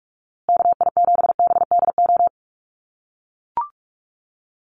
Morse code, heard as a keyed tone, about 44 wpm, 700 Hz. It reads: KI7BDO
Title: USA Call Signs Repeated 2x 45wpm